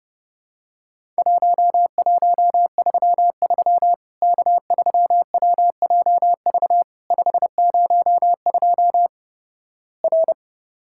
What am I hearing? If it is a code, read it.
Morse code, heard as a keyed tone, about 30 wpm, 700 Hz. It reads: 1133 K3WJV 502 R
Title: Worked All Europe DX Contest - QTC Exchange - 30wpm